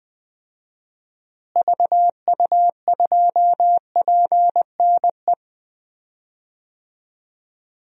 Morse code, heard as a keyed tone, about 20 wpm, 700 Hz. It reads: VU2PNE
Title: DXCC-100 Call Signs - 6 Characters - Repeated 2x - 20wpm